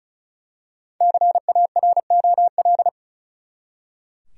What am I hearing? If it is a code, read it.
Morse code, heard as a keyed tone, about 36 wpm, 700 Hz. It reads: CAROL
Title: USA Common Names - 35wpm